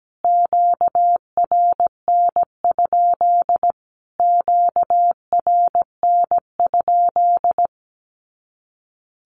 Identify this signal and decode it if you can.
Morse code, heard as a keyed tone, about 17 wpm, 700 Hz. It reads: QRN? QRN?
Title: Q-Codes Repeated 3x 17wpm